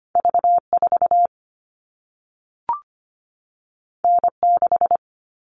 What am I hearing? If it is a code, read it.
Morse code, heard as a keyed tone, about 25 wpm, 700 Hz. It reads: V4 N6
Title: All Cut Numbers - 2-Digit 25wpm